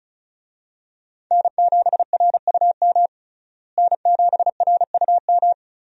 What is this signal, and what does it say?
Morse code, heard as a keyed tone, about 35 wpm, 700 Hz. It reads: N7RUM N7RUM
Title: USA Call Signs - 5 Characters - Repeated 2x - 35wpm